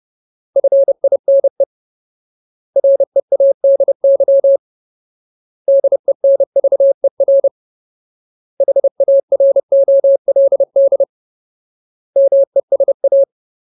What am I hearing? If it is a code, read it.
Morse code, heard as a keyed tone, about 30 wpm, 550 Hz. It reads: FINE READY DENVER HAROLD MESA
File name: Ultimate Word List / 30WPM Ultimate